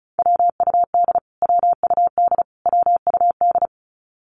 Morse code, heard as a keyed tone, about 34 wpm, 700 Hz. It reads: WUD WUD WUD
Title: Abbreviations Repeated 3x - 35wpm